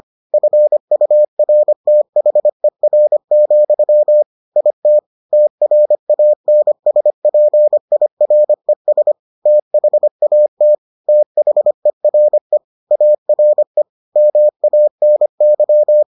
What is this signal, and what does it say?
Morse code, heard as a keyed tone, about 25 wpm, 600 Hz. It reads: FURTHER, IT TRANSPIRES THAT THERE ARE MANY